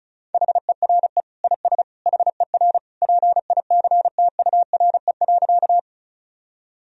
Morse code, heard as a keyed tone, about 36 wpm, 700 Hz. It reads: HERE IS HER PICTURE.
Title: Sentences from Top 1000 Words - Review Part 1 - 35wpm